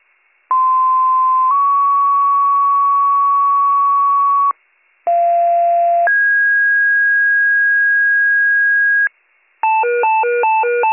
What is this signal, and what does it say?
Two-tone page, heard as a one-second tone, then a three-second tone, about 1040 then 1120 Hz, about 690 then 1650 Hz, with noise. It is heard from 0.5 to 9.1 s.